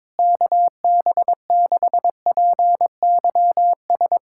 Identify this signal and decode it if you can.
Morse code, heard as a keyed tone, about 22 wpm, 700 Hz. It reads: KB6PYS
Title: USA Callsign DE Callsign - 6 Characters - 22wpm